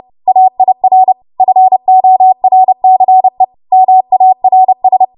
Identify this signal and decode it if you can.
Morse code, heard as a keyed tone, about 30 wpm, 750 Hz. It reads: AIR FORCE MARS
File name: Archive / 30 WPM / 30WPM